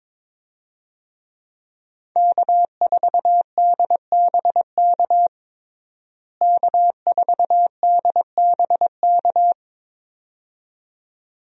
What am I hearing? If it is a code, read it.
Morse code, heard as a keyed tone, about 22 wpm, 700 Hz. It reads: K4DBK K4DBK
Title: USA Call Signs - 5 Characters - Repeated 2x - 22wpm